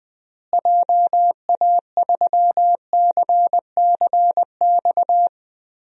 Morse code, heard as a keyed tone, about 20 wpm, 700 Hz. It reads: JA3CCX
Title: DXCC-100 Call Signs - 6 Characters - Repeated 3x - 20wpm